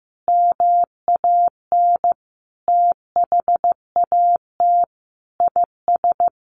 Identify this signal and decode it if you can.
Morse code, heard as a keyed tone, about 15 wpm, 700 Hz. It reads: MAN THAT IS
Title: Sentences from Top 1300 Words 15wpm